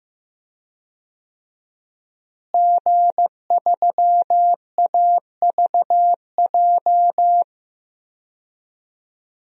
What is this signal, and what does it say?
Morse code, heard as a keyed tone, about 15 wpm, 700 Hz. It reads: G3AVJ
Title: DXCC-100 Call Signs - 5 Characters - Repeated 2x - 15wpm